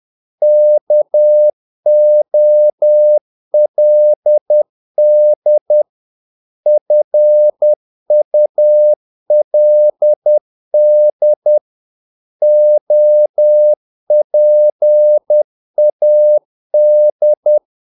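Morse code, heard as a keyed tone, about 10 wpm, 600 Hz. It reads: KOLD FULD OPAD
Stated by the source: Morse practice of short Danish words at 10 wpm, from CW med Gnister.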